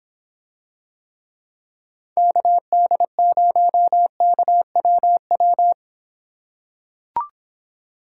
Morse code, heard as a keyed tone, about 25 wpm, 700 Hz. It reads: KD0KWW